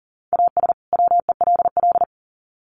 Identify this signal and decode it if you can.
Morse code, heard as a keyed tone, about 40 wpm, 700 Hz. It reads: AS WELL
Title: Sets of 2 Words 40wpm